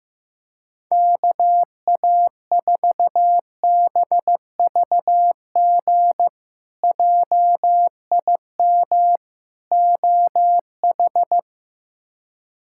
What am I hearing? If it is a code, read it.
Morse code, heard as a keyed tone, about 15 wpm, 700 Hz. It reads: KA4BVG JIM OH
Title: K1USN SST QSO Practice - 15wpm